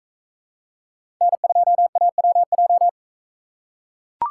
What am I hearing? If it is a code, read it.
Morse code, heard as a keyed tone, about 41 wpm, 700 Hz. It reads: N2AWJ